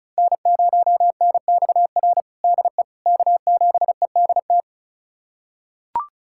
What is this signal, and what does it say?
Morse code, heard as a keyed tone, about 34 wpm, 700 Hz. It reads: N0NXR DE K7EDT
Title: USA Callsign DE Callsign - 5 Characters - 35wpm